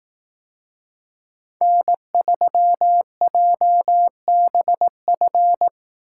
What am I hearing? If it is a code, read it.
Morse code, heard as a keyed tone, about 18 wpm, 700 Hz. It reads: N3JBF
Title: Speed Racing - USA 1x3 Call Signs 15wpm